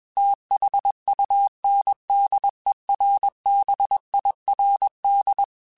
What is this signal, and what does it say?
Morse code, heard as a keyed tone, about 21 wpm, 800 Hz. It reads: THUNDERBIRD
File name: QSOs / QSOs-mp3s / WD0BGP 1@20